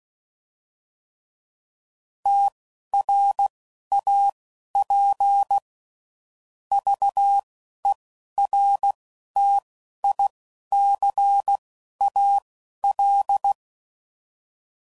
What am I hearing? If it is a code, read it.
Morse code, heard as a keyed tone, about 16 wpm, spaced as for 12 wpm, 800 Hz. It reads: TRAP VERTICAL
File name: CW Code Groups @10WPM / fromSM's / Antennas